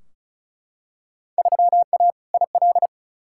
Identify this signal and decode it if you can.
Morse code, heard as a keyed tone, about 34 wpm, 700 Hz. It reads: 3A IL
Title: Field Day Exchanges - 35wpm